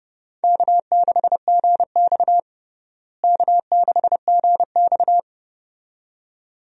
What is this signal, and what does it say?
Morse code, heard as a keyed tone, about 32 wpm, 700 Hz. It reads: K6GX K6GX
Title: USA Call Signs - 4 Characters - Repeated 2x - 30wpm